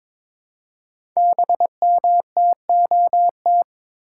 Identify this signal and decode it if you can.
Morse code, heard as a keyed tone, about 22 wpm, 690 Hz. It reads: BMTOT
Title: Code Groups - 5 Letters - 22wpm